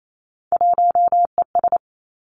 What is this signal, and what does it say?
Morse code, heard as a keyed tone, about 28 wpm, 700 Hz. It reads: 1ES
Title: Three Challenging Letter-Number Combinations 28wpm